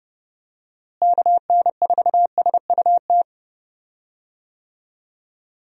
Morse code, heard as a keyed tone, about 30 wpm, 700 Hz. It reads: KN4SUT
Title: USA Call Signs - 6 Characters - Repeated 2x - 30wpm